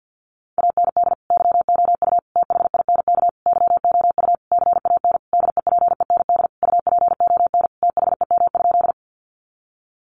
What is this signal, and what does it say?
Morse code, heard as a keyed tone, about 48 wpm, 700 Hz. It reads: AND YOU THINK YOU CAN DEPEND UPON THEM?
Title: Sentences from Top 1300 Words 50wpm